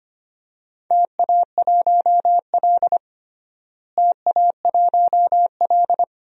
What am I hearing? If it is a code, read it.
Morse code, heard as a keyed tone, about 25 wpm, 700 Hz. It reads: TA1L TA1L
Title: DXCC-100 Call Signs - 4 Characters - Repeated 2x - 25wpm